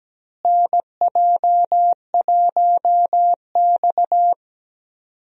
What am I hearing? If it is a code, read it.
Morse code, heard as a keyed tone, about 17 wpm, 700 Hz. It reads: NJ1X
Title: USA Call Signs - 4 Characters - Repeated 3x - 17wpm